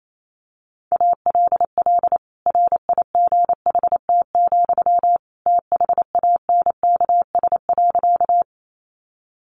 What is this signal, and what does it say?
Morse code, heard as a keyed tone, about 28 wpm, 700 Hz. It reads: ALL RIGHT, THANKS.
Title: Sentences from Top 600 Words 28wpm